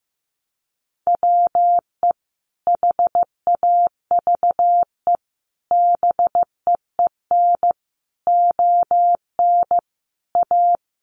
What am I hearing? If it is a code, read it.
Morse code, heard as a keyed tone, about 15 wpm, 700 Hz. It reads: WE HAVE BEEN ON A
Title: Sentences from Top 1100 Words 15wpm